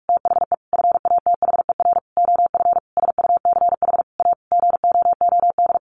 Morse code, heard as a keyed tone, about 44 wpm, 700 Hz. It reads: THE FATHER OF SUCH A GOOD
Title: Sentences from Top 1500 Words 45wpm